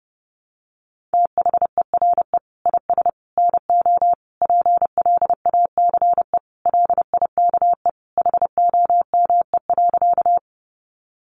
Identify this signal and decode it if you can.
Morse code, heard as a keyed tone, about 30 wpm, 700 Hz. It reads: THERE IS NO PLACE LIKE HOME.
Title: Sentences from Top 200 Words 30wpm